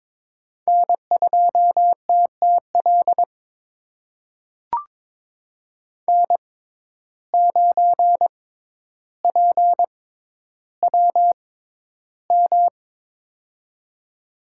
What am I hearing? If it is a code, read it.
Morse code, heard as a keyed tone, about 22 wpm, spaced as for 11 wpm, 700 Hz. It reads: N2TTL N9PWM
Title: USA Call Signs - 5 Characters - 6x Character Spacing - 22wpm